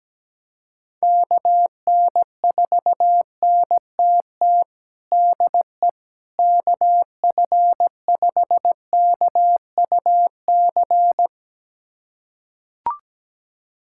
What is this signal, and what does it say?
Morse code, heard as a keyed tone, about 17 wpm, 700 Hz. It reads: KN4NTT DE KF5KUC
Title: USA Callsign DE Callsign 17wpm